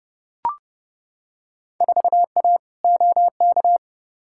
Morse code, heard as a keyed tone, about 30 wpm, 700 Hz. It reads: 4A OK
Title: Field Day Exchanges - 30wpm